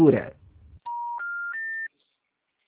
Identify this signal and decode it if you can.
Special information tone, with speech.